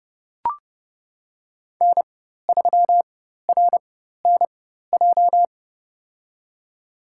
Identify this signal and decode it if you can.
Morse code, heard as a keyed tone, about 32 wpm, spaced as for 14 wpm, 700 Hz. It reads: N3RNJ
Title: USA Call Signs - 5 Characters - 4x Character Spacing - 30wpm